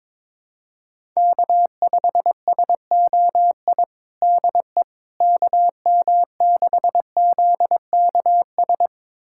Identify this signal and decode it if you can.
Morse code, heard as a keyed tone, about 22 wpm, 700 Hz. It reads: K5SOI DE KM6ZKS
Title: USA Callsign DE Callsign 22wpm